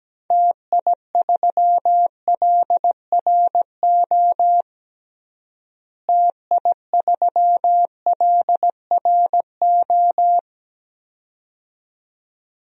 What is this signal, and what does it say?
Morse code, heard as a keyed tone, about 17 wpm, 700 Hz. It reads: TI3LRO TI3LRO
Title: DXCC-100 Call Signs - 6 Characters - Repeated 2x - 17wpm